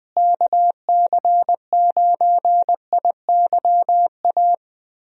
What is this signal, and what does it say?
Morse code, heard as a keyed tone, about 20 wpm, 700 Hz. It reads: KC9IYA
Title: USA Call Signs - 6 Characters - Repeated 3x - 20wpm